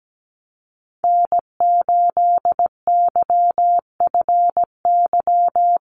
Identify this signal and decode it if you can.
Morse code, heard as a keyed tone, about 17 wpm, 700 Hz. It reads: N8YFY
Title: USA Call Signs Repeated 2x 17wpm